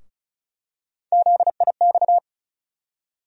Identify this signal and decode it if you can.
Morse code, heard as a keyed tone, about 36 wpm, 700 Hz. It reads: ZIX
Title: USA Call Sign Postfixes - 3 Letters 35wpm